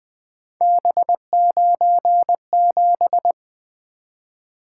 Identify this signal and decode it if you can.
Morse code, heard as a keyed tone, about 20 wpm, 700 Hz. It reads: B97
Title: Three Letter-Number Combinations - 20wpm